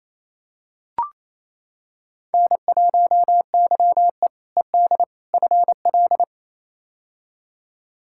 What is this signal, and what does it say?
Morse code, heard as a keyed tone, about 28 wpm, 700 Hz. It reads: N1YE ED FL
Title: K1USN SST QSO Practice - 28wpm